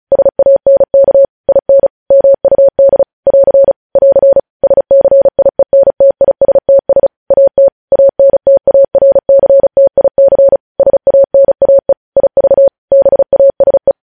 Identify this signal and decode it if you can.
Morse code, heard as a keyed tone, about 36 wpm, 550 Hz. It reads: SANK IN MUD <AR> <AR> SCIENTISTS AT ANTARCTIC SANAE IV BASE
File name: podcast / News headlines-1x-35x35